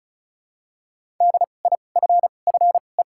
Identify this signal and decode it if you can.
Morse code, heard as a keyed tone, about 36 wpm, spaced as for 25 wpm, 700 Hz. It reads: DIFFE
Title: 2x Character Spacing - Top 300 Words - 35wpm